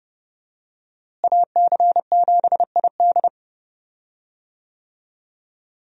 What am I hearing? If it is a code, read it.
Morse code, heard as a keyed tone, about 31 wpm, 700 Hz. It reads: AC7ID
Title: USA Call Signs - 5 Characters - Repeated 2x - 30wpm